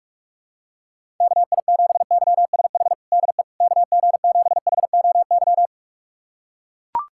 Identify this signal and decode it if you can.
Morse code, heard as a keyed tone, about 43 wpm, 700 Hz. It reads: KI7YSH DE KG7HOY
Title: USA Callsign DE Callsign 45wpm